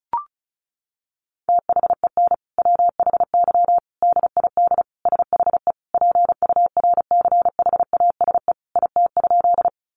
Morse code, heard as a keyed tone, about 34 wpm, 700 Hz. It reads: THEN WHY DID SHE PURCHASE IT?
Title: Sentences from Top 800 Words 35wpm